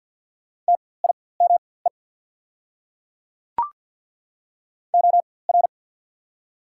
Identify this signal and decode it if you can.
Morse code, heard as a keyed tone, about 48 wpm, spaced as for 23 wpm, 700 Hz. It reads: TIME OR
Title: 4x Character Spacing - Top 100 Words - 50wpm